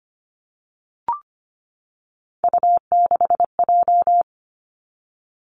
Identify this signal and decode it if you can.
Morse code, heard as a keyed tone, about 25 wpm, 700 Hz. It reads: U6J
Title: Three Challenging Letter-Number Combinations 25wpm